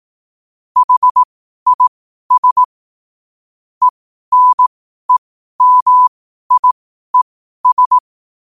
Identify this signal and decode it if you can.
Morse code, heard as a keyed tone, about 18 wpm, spaced as for 12 wpm, 1000 Hz. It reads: HIS ENEMIES
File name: qotdfiles / 12wpm